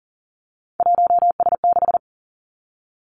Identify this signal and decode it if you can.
Morse code, heard as a keyed tone, about 40 wpm, 700 Hz. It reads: 1S6
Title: Three Challenging Letter-Number Combinations 40wpm